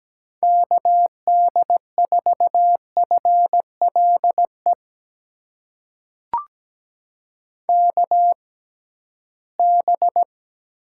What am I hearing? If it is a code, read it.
Morse code, heard as a keyed tone, about 17 wpm, spaced as for 13 wpm, 700 Hz. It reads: KD4FLE KB
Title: USA Call Signs - 6x Character Spacing - 17wpm